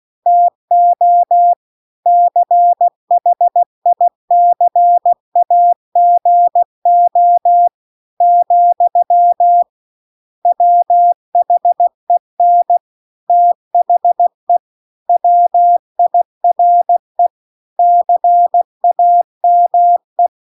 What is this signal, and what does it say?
Morse code, heard as a keyed tone, about 16 wpm, 700 Hz. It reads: TO CHICAGO , WHEN THE WIRE CAME